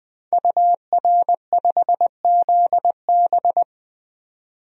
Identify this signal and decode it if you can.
Morse code, heard as a keyed tone, about 21 wpm, 700 Hz. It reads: UR5ZB
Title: DXCC-100 Call Signs - 5 Characters - 20wpm